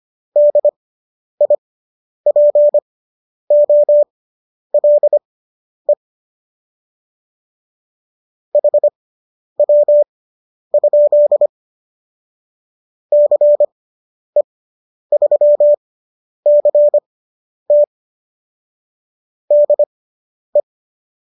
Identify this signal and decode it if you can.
Morse code, heard as a keyed tone, about 25 wpm, spaced as for 10 wpm, 590 Hz. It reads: DIPOLE HW? CE3CT DE